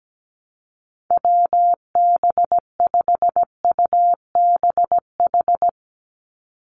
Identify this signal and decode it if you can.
Morse code, heard as a keyed tone, about 17 wpm, 700 Hz. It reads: WB5UBH
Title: USA Call Signs 17wpm